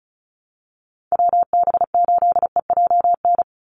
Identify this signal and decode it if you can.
Morse code, heard as a keyed tone, about 34 wpm, 700 Hz. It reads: WB8EJN
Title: Call Signs Repeated 2x - 35wpm